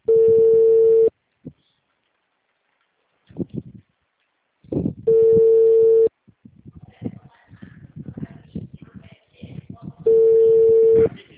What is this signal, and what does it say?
Ringback tone (China, 450 Hz), with speech.